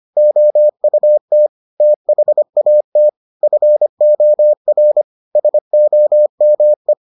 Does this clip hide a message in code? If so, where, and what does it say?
Yes: Morse code from 0.2 to 6.9 s, heard as a keyed tone, about 25 wpm, 600 Hz. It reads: OUT THAT FOR SOME